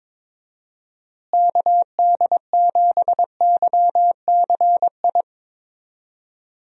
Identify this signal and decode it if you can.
Morse code, heard as a keyed tone, about 21 wpm, 700 Hz. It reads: KD7YCI
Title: USA Call Signs - 6 Characters - Repeated 2x - 22wpm